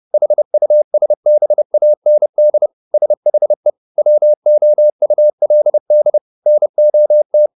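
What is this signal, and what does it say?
Morse code, heard as a keyed tone, about 30 wpm, 600 Hz. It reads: HUSBAND SHE WOULD NOT
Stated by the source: Morse code at 30 wpm, the novel Never by Ken Follett in English, from CW med Gnister.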